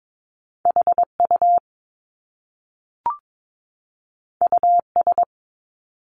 Morse code, heard as a keyed tone, about 22 wpm, 700 Hz. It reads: HU US
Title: Top Two Letter Combinations in Words 22wpm